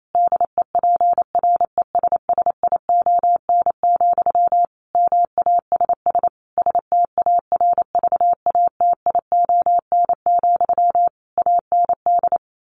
Morse code, heard as a keyed tone, about 28 wpm, 700 Hz. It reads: DEPRESSION, MASS STARVATION, AND
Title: A Sound Of Thunder 28wpm